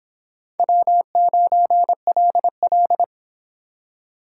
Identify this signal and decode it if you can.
Morse code, heard as a keyed tone, about 25 wpm, 700 Hz. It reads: W9LL